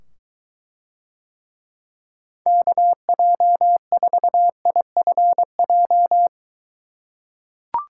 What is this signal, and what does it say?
Morse code, heard as a keyed tone, about 23 wpm, 700 Hz. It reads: KJ4IFJ